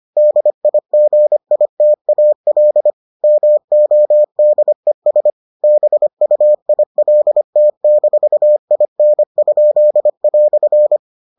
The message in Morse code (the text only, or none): DIGITAL MODES BUILT-IN?"